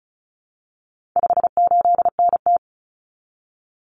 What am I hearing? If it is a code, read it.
Morse code, heard as a keyed tone, about 34 wpm, 700 Hz. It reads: 58NT
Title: Common Cut Numbers - 4-Digit 35wpm